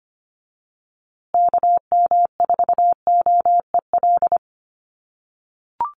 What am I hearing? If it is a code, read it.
Morse code, heard as a keyed tone, about 25 wpm, 700 Hz. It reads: KM4OEL
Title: USA Call Signs Repeated 2x 25wpm